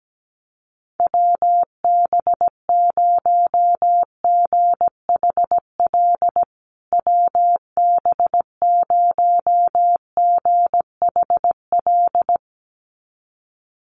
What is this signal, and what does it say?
Morse code, heard as a keyed tone, about 17 wpm, 700 Hz. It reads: WB0GHL WB0GHL
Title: USA Call Signs Repeated 2x 17wpm